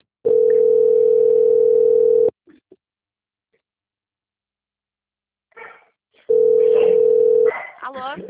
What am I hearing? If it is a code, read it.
Ringback tone (North America), 440 + 480 Hz, with speech.
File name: out- - - - - .93107